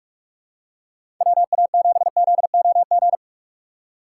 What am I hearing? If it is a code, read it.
Morse code, heard as a keyed tone, about 44 wpm, 700 Hz. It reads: WA7ZOG
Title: USA Call Signs - 6 Characters - 2x Character Spacing - 45wpm